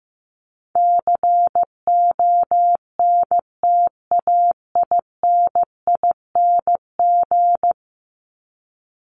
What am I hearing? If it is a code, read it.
Morse code, heard as a keyed tone, about 15 wpm, 700 Hz. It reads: CONTAINING
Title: ING Suffix - Top 100 Words with Base Words 15wpm